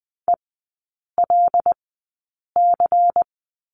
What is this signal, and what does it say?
Morse code, heard as a keyed tone, about 20 wpm, spaced as for 8 wpm, 700 Hz. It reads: ELC
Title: 107 Lesson 18e - Letter m - Three Characters 20wpm